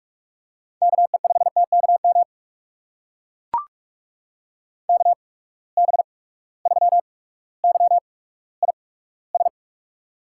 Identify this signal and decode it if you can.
Morse code, heard as a keyed tone, about 43 wpm, spaced as for 19 wpm, 700 Hz. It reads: KE5TKM KB3YIS
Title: USA Call Signs - 6 Characters - 8x Character Spacing - 45wpm